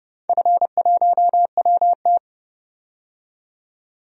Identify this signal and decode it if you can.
Morse code, heard as a keyed tone, about 30 wpm, 700 Hz. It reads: F1WT
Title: DXCC-100 Call Signs - 4 Characters - 30wpm